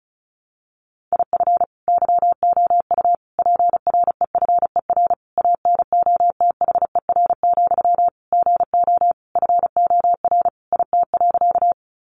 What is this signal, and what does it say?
Morse code, heard as a keyed tone, about 34 wpm, 700 Hz. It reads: IF YOU PREFER ANOTHER, GO FOR IT.
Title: Sentences from Top 1100 Words 35wpm